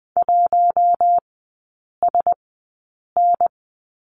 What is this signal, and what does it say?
Morse code, heard as a keyed tone, about 20 wpm, spaced as for 8 wpm, 700 Hz. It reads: 1SN
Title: 053 Lesson 10e - Letter r - Three Characters 20wpm